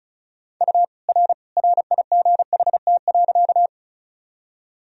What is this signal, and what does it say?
Morse code, heard as a keyed tone, about 36 wpm, 700 Hz. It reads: U R RIGHT.
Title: Sentences from Top 200 Words and Abbreviations 35wpm